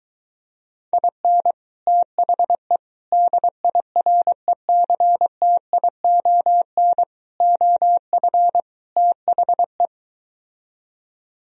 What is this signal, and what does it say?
Morse code, heard as a keyed tone, about 23 wpm, 700 Hz. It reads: IN THE DIRECTION OF THE